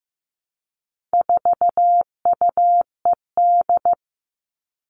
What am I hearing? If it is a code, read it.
Morse code, heard as a keyed tone, about 15 wpm, 700 Hz. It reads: 4UED
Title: All Cut Numbers - 4-Digit 15wpm